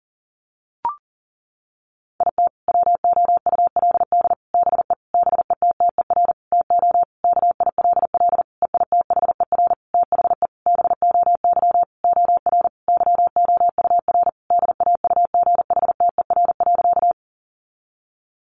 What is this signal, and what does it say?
Morse code, heard as a keyed tone, about 40 wpm, 700 Hz. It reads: IT WOULD BE BETTER TO KILL EITHER THE BOY OR YOUR DAUGHTER.
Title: Sentences from Top 1500 Words 40wpm